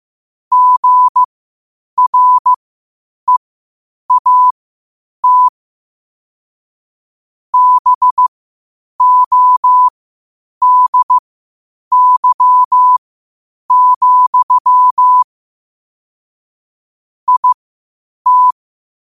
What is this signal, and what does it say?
Morse code, heard as a keyed tone, about 15 wpm, spaced as for 8 wpm, 1000 Hz. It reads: GREAT BODY, IT
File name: qotdfiles / 8wpm